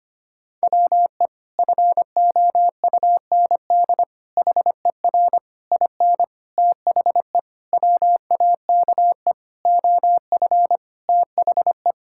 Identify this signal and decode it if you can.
Morse code, heard as a keyed tone, about 25 wpm, 700 Hz. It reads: WE FOUND HER IN THE WAKE OF THE
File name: practice / 25wpm / Sentences with 4-Word Phrases - 25wpm